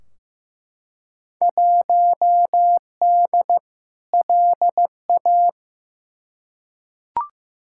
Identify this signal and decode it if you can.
Morse code, heard as a keyed tone, about 15 wpm, 700 Hz. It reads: 1D LA
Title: Field Day Exchanges - 15wpm